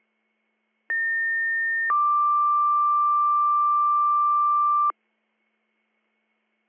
Two-tone page, heard as a one-second tone, then a three-second tone, about 1800 then 1180 Hz. It is heard from 0.9 to 4.9 s.